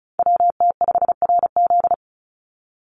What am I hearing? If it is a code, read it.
Morse code, heard as a keyed tone, about 34 wpm, 700 Hz. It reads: WT5RZ
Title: Call Signs Repeated 3x - 35wpm